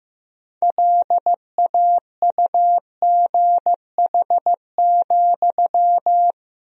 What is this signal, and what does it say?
Morse code, heard as a keyed tone, about 16 wpm, 700 Hz. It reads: LAUGH,
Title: 4x Word-Spacing - Sentences from Top 500 Words - 15wpm